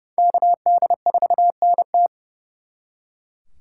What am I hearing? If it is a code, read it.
Morse code, heard as a keyed tone, about 40 wpm, spaced as for 35 wpm, 700 Hz. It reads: KD4NT
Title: Speed Racing - USA 2x2 Call Signs 30wpm